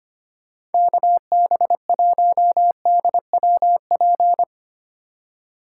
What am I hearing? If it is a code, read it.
Morse code, heard as a keyed tone, about 31 wpm, 700 Hz. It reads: KB1DWP
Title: Speed Racing - USA Call Signs 25wpm